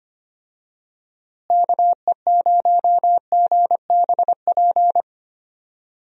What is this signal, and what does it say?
Morse code, heard as a keyed tone, about 25 wpm, 700 Hz. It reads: KE0GBP
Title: USA Call Signs - 6 Characters - Repeated 3x - 25wpm